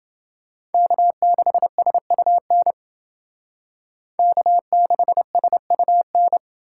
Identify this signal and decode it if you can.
Morse code, heard as a keyed tone, about 29 wpm, 700 Hz. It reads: K6SUN K6SUN